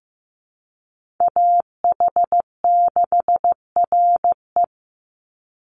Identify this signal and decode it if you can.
Morse code, heard as a keyed tone, about 15 wpm, 700 Hz. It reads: AH6RE
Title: Call Signs Repeated 3x - 15wpm